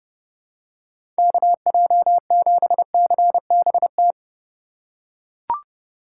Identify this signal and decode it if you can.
Morse code, heard as a keyed tone, about 30 wpm, 700 Hz. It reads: KJ7CBT
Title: USA Call Signs - 6 Characters - 3x Character Spacing - 30wpm